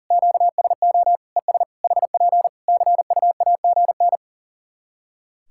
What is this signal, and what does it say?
Morse code, heard as a keyed tone, about 40 wpm, 700 Hz. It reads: QSO ES HP CUAGN
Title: K1USN SST QSO Practice 40wpm